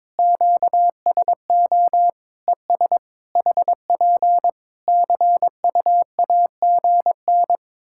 Morse code, heard as a keyed tone, about 22 wpm, 700 Hz. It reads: QSO ES HP CUAGN
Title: K1USN SST QSO Practice 22wpm